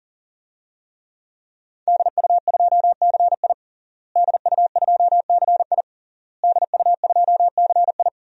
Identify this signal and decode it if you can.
Morse code, heard as a keyed tone, about 40 wpm, 700 Hz. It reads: DU2CI DU2CI DU2CI
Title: DXCC-100 Call Signs - 5 Characters - Repeated 3x - 40wpm